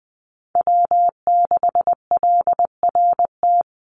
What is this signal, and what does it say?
Morse code, heard as a keyed tone, about 20 wpm, 700 Hz. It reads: W6LRT
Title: USA Call Signs 20wpm